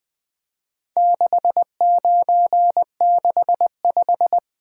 Morse code, heard as a keyed tone, about 22 wpm, 700 Hz. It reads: 6965
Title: Speed-Racing - Four Letter-Number Combinations - 20wpm